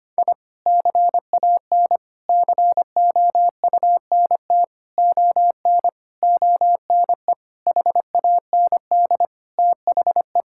Morse code, heard as a keyed tone, about 25 wpm, 700 Hz. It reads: I CAN COUNT ON ONE HAND THE
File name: practice / 25wpm / Sentences with 9-Word Phrases - 25wpm